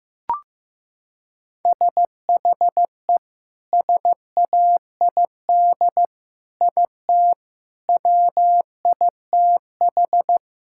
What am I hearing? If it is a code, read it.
Morse code, heard as a keyed tone, about 15 wpm, 700 Hz. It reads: SHE SAID IT WITH
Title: Sentences from Top 1600 Words 15wpm